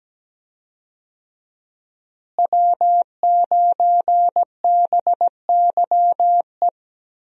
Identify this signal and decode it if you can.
Morse code, heard as a keyed tone, about 17 wpm, 700 Hz. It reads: W9BYE
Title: USA Call Signs - 5 Characters - Repeated 3x - 17wpm